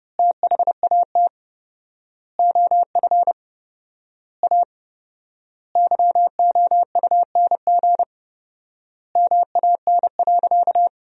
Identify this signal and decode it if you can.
Morse code, heard as a keyed tone, about 32 wpm, 700 Hz. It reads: THAT OF A YOUNG MAN.
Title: 4x Word-Spacing - Sentences from Top 500 Words - 30wpm